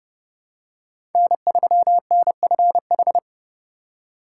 Morse code, heard as a keyed tone, about 30 wpm, 700 Hz. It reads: N3NFH
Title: USA Call Signs - 4x Character Spacing - 30wpm